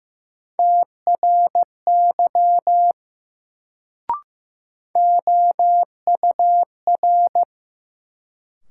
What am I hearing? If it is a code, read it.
Morse code, heard as a keyed tone, about 15 wpm, 700 Hz. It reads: TRY OUR
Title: Sentences from Top 200 Words - New Word Drill - 15wpm